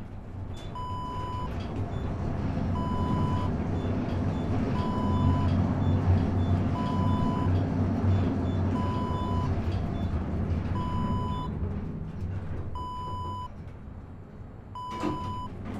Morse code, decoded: none